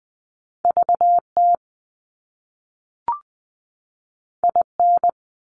Morse code, heard as a keyed tone, about 20 wpm, 700 Hz. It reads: VT IN
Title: US States 20wpm